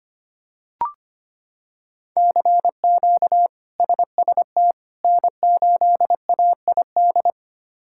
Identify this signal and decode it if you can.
Morse code, heard as a keyed tone, about 25 wpm, 700 Hz. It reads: CQ SST N8AID
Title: K1USN SST QSO Practice - 25wpm